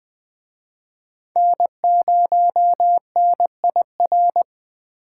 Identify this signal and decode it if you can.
Morse code, heard as a keyed tone, about 20 wpm, 700 Hz. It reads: N0NIR
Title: Speed Racing - USA Call Signs 17wpm